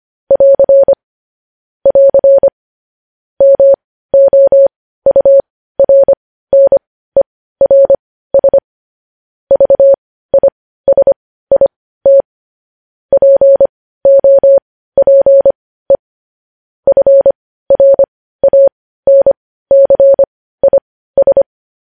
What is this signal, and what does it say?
Morse code, heard as a keyed tone, about 25 wpm, spaced as for 15 wpm, 550 Hz. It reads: <AR> <AR> MOURNERS VISIT POPE FRANCIS
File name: podcast / News headlines-1x-25x15